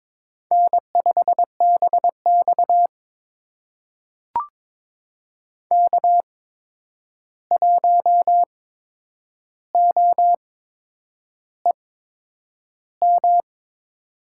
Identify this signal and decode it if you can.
Morse code, heard as a keyed tone, about 22 wpm, spaced as for 9 wpm, 700 Hz. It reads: N5BX K1OEM
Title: USA Call Signs - 8x Character Spacing - 22wpm